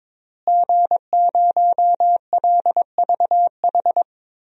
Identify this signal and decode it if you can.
Morse code, heard as a keyed tone, about 22 wpm, 700 Hz. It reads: G0LVH
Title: DXCC-100 Call Signs - 5 Characters - Repeated 2x - 22wpm